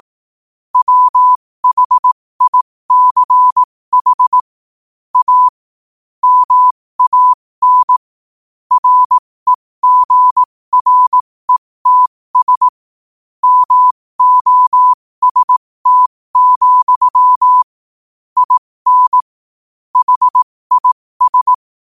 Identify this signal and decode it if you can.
Morse code, heard as a keyed tone, about 18 wpm, spaced as for 15 wpm, 1000 Hz. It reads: WHICH A MAN REGRETS MOST, IN HIS
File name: qotdfiles / 15wpm